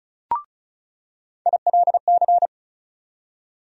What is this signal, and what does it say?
Morse code, heard as a keyed tone, about 36 wpm, 700 Hz. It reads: ILC
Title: USA Call Sign Postfixes - 3 Letters 35wpm